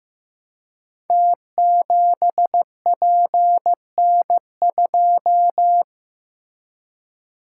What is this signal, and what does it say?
Morse code, heard as a keyed tone, about 15 wpm, 700 Hz. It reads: T7PN2
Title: Code Groups - 5 Letter:Numbers - 15wpm